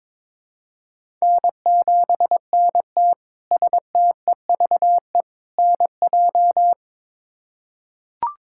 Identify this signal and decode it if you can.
Morse code, heard as a keyed tone, about 22 wpm, 700 Hz. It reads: N7NT STEVE NJ
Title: K1USN SST QSO Practice - 22wpm